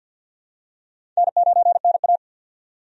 Morse code, heard as a keyed tone, about 47 wpm, 700 Hz. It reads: N9NA
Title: USA Call Signs - 4 Characters - 50wpm